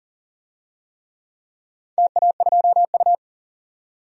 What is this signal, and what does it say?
Morse code, heard as a keyed tone, about 42 wpm, 700 Hz. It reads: TA2U
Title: DXCC-100 Call Signs - 4 Characters - 40wpm